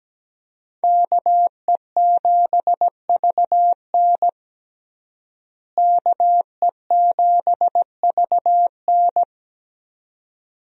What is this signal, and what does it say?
Morse code, heard as a keyed tone, about 17 wpm, 700 Hz. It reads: KE7VN KE7VN
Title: USA Call Signs - 5 Characters - Repeated 2x - 17wpm